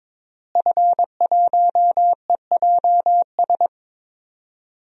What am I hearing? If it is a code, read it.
Morse code, heard as a keyed tone, about 22 wpm, 700 Hz. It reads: F1EJS
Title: DXCC-100 Call Signs - 5 Characters - Repeated 3x - 22wpm